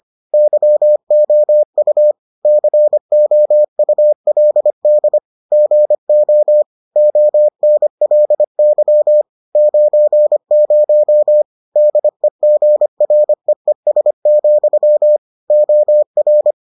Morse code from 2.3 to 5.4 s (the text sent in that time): COULD